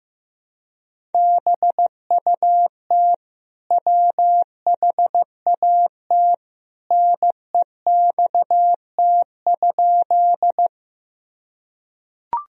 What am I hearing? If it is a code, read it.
Morse code, heard as a keyed tone, about 16 wpm, 700 Hz. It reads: BUT WHAT NEXT?
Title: Sentences from Top 1000 Words - Review Part 1 - 15wpm